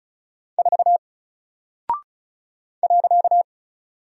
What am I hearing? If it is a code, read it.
Morse code, heard as a keyed tone, about 36 wpm, 700 Hz. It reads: 4 .
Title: Single Character (Letter:Number:Punctuation:Prosign) 35wpm